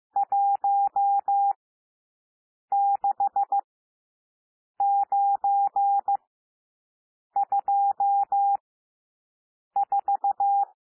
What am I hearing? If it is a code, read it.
Morse code, heard as a keyed tone, about 15 wpm, spaced as for 6 wpm, 800 Hz. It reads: 16924
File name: cwmp3 / N1 SOUND NUMBERS 15 wpm